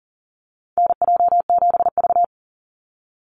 Morse code, heard as a keyed tone, about 41 wpm, 700 Hz. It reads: NJ7V
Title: NA SOTA Chaser Callsigns 40wpm